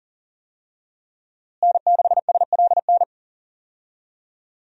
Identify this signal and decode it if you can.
Morse code, heard as a keyed tone, about 40 wpm, 700 Hz. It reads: N6SLN
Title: USA Call Signs - 5 Characters - Repeated 2x - 40wpm